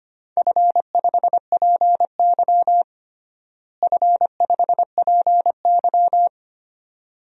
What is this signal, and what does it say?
Morse code, heard as a keyed tone, about 25 wpm, 700 Hz. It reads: F5PY F5PY
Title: DXCC-100 Call Signs - 4 Characters - Repeated 2x - 25wpm